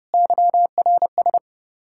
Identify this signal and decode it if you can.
Morse code, heard as a keyed tone, about 39 wpm, 700 Hz. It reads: YRS